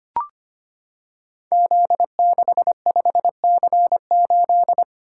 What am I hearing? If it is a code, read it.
Morse code, heard as a keyed tone, about 25 wpm, 700 Hz. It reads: Z65C8
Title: Code Groups - 5 Letter:Numbers - 25wpm